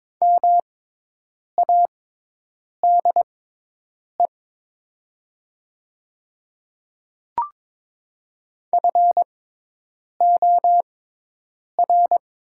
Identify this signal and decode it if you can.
Morse code, heard as a keyed tone, about 22 wpm, spaced as for 8 wpm, 700 Hz. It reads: MADE FOR
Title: 6x Character Spacing - Top 200 Words - 22wpm